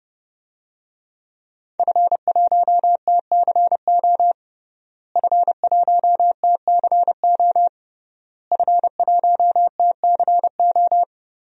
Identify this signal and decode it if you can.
Morse code, heard as a keyed tone, about 30 wpm, 700 Hz. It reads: F1TCO F1TCO F1TCO
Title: DXCC-100 Call Signs - 5 Characters - Repeated 3x - 30wpm